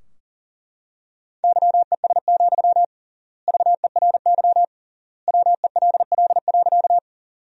Morse code, heard as a keyed tone, about 40 wpm, 700 Hz. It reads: YES, VERY WELL.
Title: 3x Word-Spacing - Sentences from Top 500 Words - 40wpm